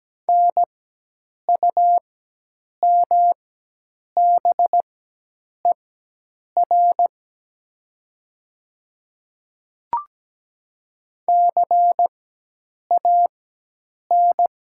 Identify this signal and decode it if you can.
Morse code, heard as a keyed tone, about 17 wpm, spaced as for 8 wpm, 700 Hz. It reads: NUMBER CAN
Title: 4x Character Spacing - Top 200 Words - 17wpm